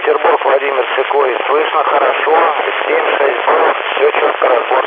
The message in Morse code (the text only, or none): none